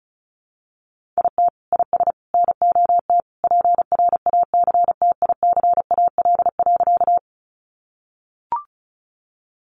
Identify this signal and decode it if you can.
Morse code, heard as a keyed tone, about 34 wpm, 700 Hz. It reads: IT IS NOT PRACTICAL.
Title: Sentences from Top 1400 Words 35wpm